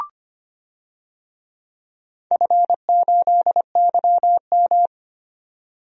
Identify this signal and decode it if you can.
Morse code, heard as a keyed tone, about 25 wpm, 700 Hz. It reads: F8YM
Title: DXCC-100 Call Signs - 4 Characters - 25wpm